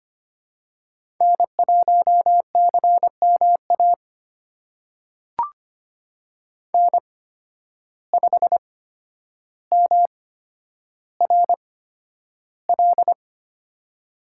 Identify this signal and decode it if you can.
Morse code, heard as a keyed tone, about 25 wpm, spaced as for 11 wpm, 700 Hz. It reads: N1CMA N5MRL
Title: USA Call Signs - 5 Characters - 8x Character Spacing - 25wpm